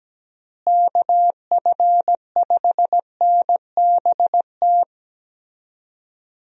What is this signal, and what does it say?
Morse code, heard as a keyed tone, about 17 wpm, 700 Hz. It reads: KF5NBT
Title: USA Call Signs - 6 Characters - Repeated 3x - 17wpm